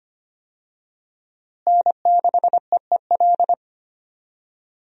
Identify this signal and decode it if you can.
Morse code, heard as a keyed tone, about 25 wpm, 700 Hz. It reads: N6EEL
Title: USA Call Signs - 5 Characters - Repeated 2x - 25wpm